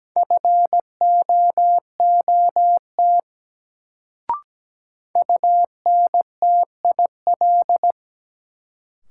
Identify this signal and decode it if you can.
Morse code, heard as a keyed tone, about 17 wpm, 700 Hz. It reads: FOOT UNTIL
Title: Sentences from Top 500 Words - New Word Drill - 17wpm